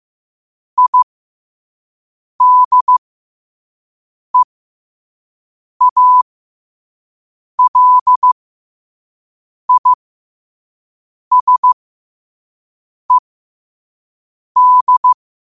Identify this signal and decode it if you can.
Morse code, heard as a keyed tone, about 15 wpm, spaced as for 5 wpm, 1000 Hz. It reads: IDEALISED
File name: qotdfiles / 5wpm